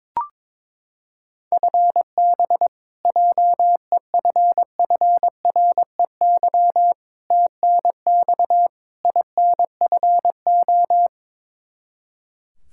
Morse code, heard as a keyed tone, about 21 wpm, 700 Hz. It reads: FB JEFFREY TNX INFO
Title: K1USN SST QSO Practice 22wpm